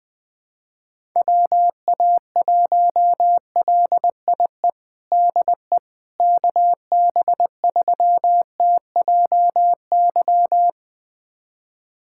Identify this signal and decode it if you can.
Morse code, heard as a keyed tone, about 20 wpm, 700 Hz. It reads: WA1LIE DE KB3TJY
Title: USA Callsign DE Callsign - 6 Characters - 20wpm